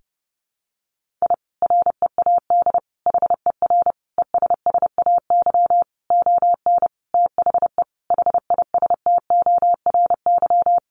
Morse code, heard as a keyed tone, about 30 wpm, 700 Hz. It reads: I READ HER ESSAY ON THE HISTORY
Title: Sentences from Top 1500 Words 30wpm